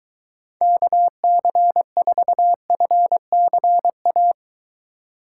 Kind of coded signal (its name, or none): Morse code